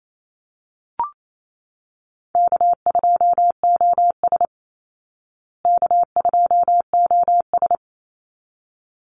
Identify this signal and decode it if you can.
Morse code, heard as a keyed tone, about 28 wpm, 700 Hz. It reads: K2OS K2OS
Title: Easy Call Signs Repeated 2x - 28wpm